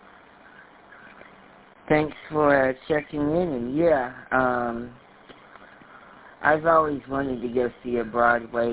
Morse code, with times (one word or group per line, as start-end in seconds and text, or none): none